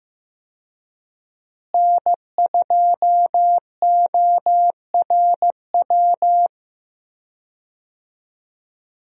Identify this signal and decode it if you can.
Morse code, heard as a keyed tone, about 15 wpm, 700 Hz. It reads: N2ORW
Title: USA Call Signs - 5 Characters - Repeated 3x - 15wpm